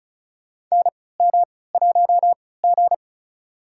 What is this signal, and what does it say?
Morse code, heard as a keyed tone, about 36 wpm, spaced as for 20 wpm, 700 Hz. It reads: NM1G
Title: USA Call Signs - 4 Characters - 3x Character Spacing - 35wpm